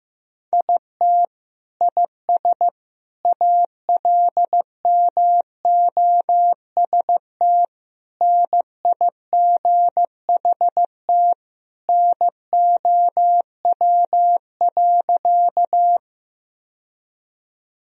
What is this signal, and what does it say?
Morse code, heard as a keyed tone, about 15 wpm, 700 Hz. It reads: IT IS ALMOST NIGHT NOW.
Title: Sentences from Top 1000 Words - Review Part 4 - 15wpm